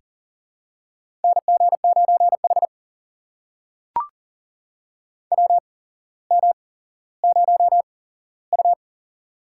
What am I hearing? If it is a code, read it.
Morse code, heard as a keyed tone, about 40 wpm, spaced as for 17 wpm, 700 Hz. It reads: NG9H WM0U
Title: USA Call Signs - 4 Characters - 8x Character Spacing - 40wpm